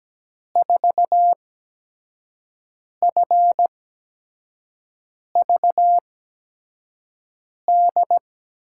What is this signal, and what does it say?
Morse code, heard as a keyed tone, about 17 wpm, spaced as for 5 wpm, 700 Hz. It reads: 4FVD
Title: USA Call Signs - 6 Characters - 8x Character Spacing - 17wpm